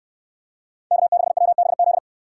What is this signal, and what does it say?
Morse code, heard as a keyed tone, about 68 wpm, 700 Hz. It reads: K6WBZ